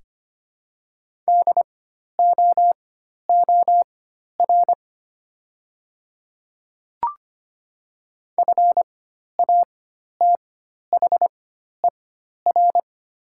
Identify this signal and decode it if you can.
Morse code, heard as a keyed tone, about 25 wpm, spaced as for 12 wpm, 700 Hz. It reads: DOOR FATHER
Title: 4x Character Spacing - Top 500 Words - 25wpm